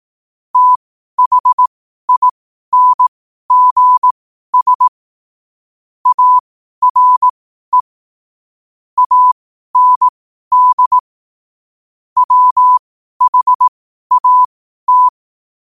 Morse code, heard as a keyed tone, about 18 wpm, spaced as for 12 wpm, 1000 Hz. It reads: THINGS ARE AND WHAT
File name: qotdfiles / 12wpm